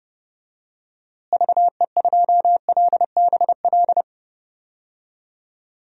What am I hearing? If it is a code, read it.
Morse code, heard as a keyed tone, about 30 wpm, 700 Hz. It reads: VE2LBL
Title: DXCC-100 Call Signs - 6 Characters - Repeated 3x - 30wpm